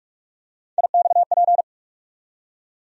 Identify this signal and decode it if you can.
Morse code, heard as a keyed tone, about 42 wpm, 700 Hz. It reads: IXP
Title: USA Call Sign Postfixes - 3 Letters 45wpm